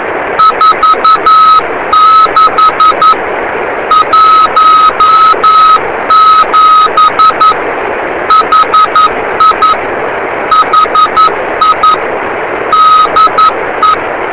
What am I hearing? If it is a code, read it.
Morse code, heard as a keyed tone, about 11 wpm, 1280 Hz. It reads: 46 17 HI HI DE